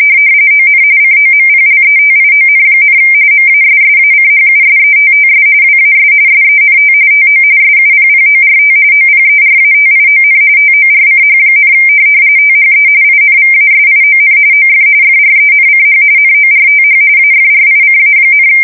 RTTY, about 45.45 baud. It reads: ON COMPLETION, MASSACHUSETTS WAS SENT TO SUPPORT OPERATION TORCH, THE INVASION OF FRENCH NORTH AFRICA,
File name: fsk170 / USS-Massachusetts